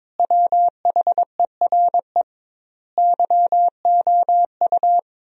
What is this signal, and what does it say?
Morse code, heard as a keyed tone, about 21 wpm, 700 Hz. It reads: WHERE YOU
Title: 2x Word-Spacing - Sentences from Top 200 Words - 22wpm